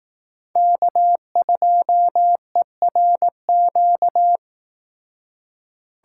Morse code, heard as a keyed tone, about 19 wpm, 700 Hz. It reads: K2ERQ
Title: Speed Racing - USA Call Signs 15wpm